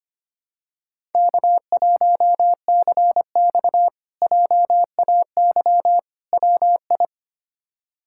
Morse code, heard as a keyed tone, about 25 wpm, 700 Hz. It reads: K1CX JAY WI
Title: K1USN SST QSO Practice - 25wpm